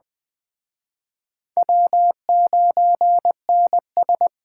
Morse code, heard as a keyed tone, about 20 wpm, 700 Hz. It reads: W9NS
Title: Speed Racing - USA 1x2 Call Signs 17wpm